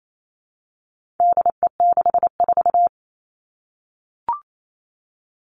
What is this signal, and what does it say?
Morse code, heard as a keyed tone, about 28 wpm, 700 Hz. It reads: DE64
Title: All Cut Numbers - 4-Digit 28wpm